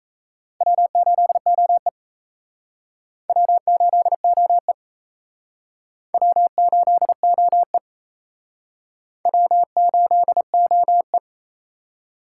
Morse code, heard as a keyed tone, about 34 wpm, 700 Hz. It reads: W8OE W8OE W8OE W8OE
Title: Speed Racing - USA 1x2 Call Signs 28wpm